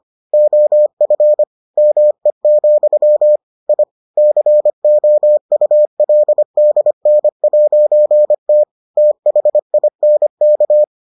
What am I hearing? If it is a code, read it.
Morse code, heard as a keyed tone, about 25 wpm, 600 Hz. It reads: OF ME, I COULDN'T THINK